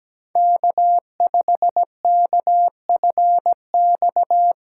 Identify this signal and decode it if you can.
Morse code, heard as a keyed tone, about 17 wpm, 700 Hz. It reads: K5KFX
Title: USA Callsign DE Callsign - 5 Characters - 17wpm